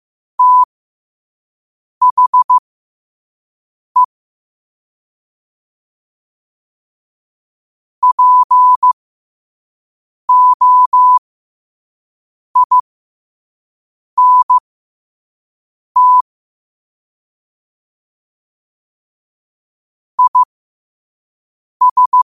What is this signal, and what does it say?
Morse code, heard as a keyed tone, about 15 wpm, spaced as for 5 wpm, 1000 Hz. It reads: THE POINT IS
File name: qotdfiles / 5wpm